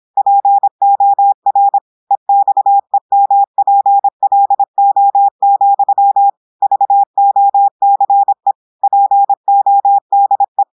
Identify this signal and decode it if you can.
Morse code, heard as a keyed tone, about 26 wpm, 800 Hz. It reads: POR EXEMPLO, VOCE PODE
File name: portugais texte en clair 2 vitesse 26 mots minutes